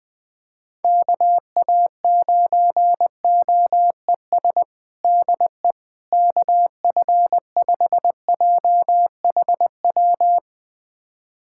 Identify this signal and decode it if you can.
Morse code, heard as a keyed tone, about 20 wpm, 700 Hz. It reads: KA9OES DE KF5JHW
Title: USA Callsign DE Callsign 20wpm